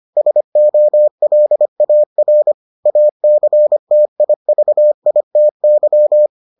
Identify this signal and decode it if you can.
Morse code, heard as a keyed tone, about 25 wpm, 600 Hz. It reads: SOLAR ACTIVITY